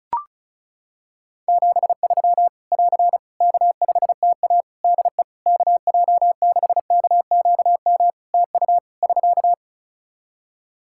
Morse code, heard as a keyed tone, about 36 wpm, 700 Hz. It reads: 73 <AR> K5TA DE KJ6KQM TU <SK>
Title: Practice QSO - Exchange 4 - Ending - 35wpm